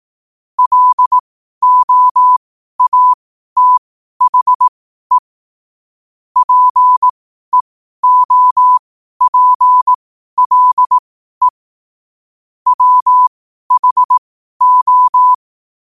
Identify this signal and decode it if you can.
Morse code, heard as a keyed tone, about 18 wpm, spaced as for 12 wpm, 1000 Hz. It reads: LOATHE PEOPLE WHO
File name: qotdfiles / 12wpm